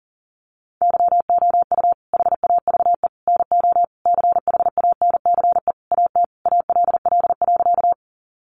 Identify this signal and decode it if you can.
Morse code, heard as a keyed tone, about 40 wpm, 700 Hz. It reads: YOU HAVE NO CHANCE AT ALL.
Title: Sentences from Top 800 Words 40wpm